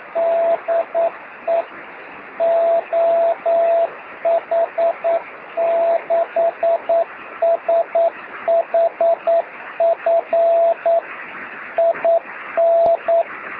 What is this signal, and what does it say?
Morse code, heard as a keyed tone, about 9 wpm, 670 Hz, with radio noise. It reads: DE OH6SHF IN
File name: sm7lcb / 2109z oh6shf 23cm beacon